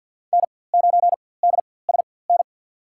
Morse code, heard as a keyed tone, about 48 wpm, spaced as for 23 wpm, 700 Hz. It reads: N9DSN
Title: USA Call Signs - 5 Characters - 4x Character Spacing - 50wpm